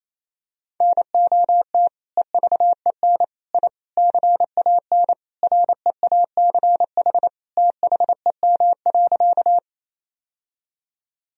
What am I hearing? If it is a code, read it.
Morse code, heard as a keyed tone, about 28 wpm, 700 Hz. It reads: NOT EVEN I CAN REACH THEM.
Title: Sentences from Top 1000 Words - Review Part 2 - 28wpm